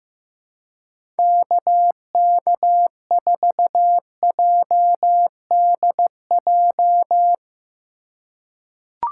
Morse code, heard as a keyed tone, about 16 wpm, 700 Hz. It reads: KK4JDJ
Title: USA Call Signs - 6 Characters - 4x Character Spacing - 15wpm